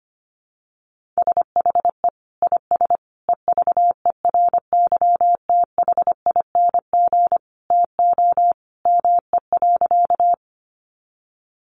Morse code, heard as a keyed tone, about 25 wpm, 700 Hz. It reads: SHE IS EVERYTHING TO ME.
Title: Sentences from Top 700 Words 25wpm